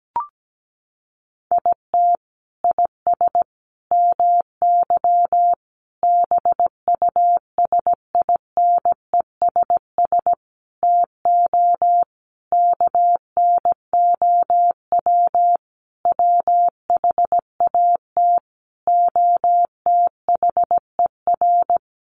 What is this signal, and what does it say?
Morse code, heard as a keyed tone, about 17 wpm, 700 Hz. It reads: IT IS MY BUSINESS TO KNOW WHAT OTHER
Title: Sentences from Top 600 Words 17wpm